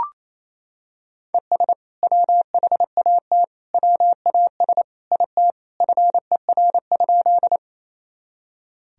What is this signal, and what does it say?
Morse code, heard as a keyed tone, about 28 wpm, 700 Hz. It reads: ES WHAT WAS IT FER?
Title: Sentences from Top 100 Words and Abbreviations 28wpm